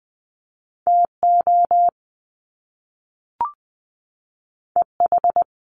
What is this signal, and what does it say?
Morse code, heard as a keyed tone, about 20 wpm, 700 Hz. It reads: T O E H
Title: Top Two Letter Combinations in Words 20wpm